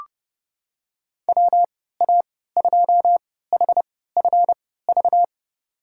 Morse code, heard as a keyed tone, about 31 wpm, spaced as for 17 wpm, 700 Hz. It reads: WA2HFV
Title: USA Call Signs - 6 Characters - 3x Character Spacing - 30wpm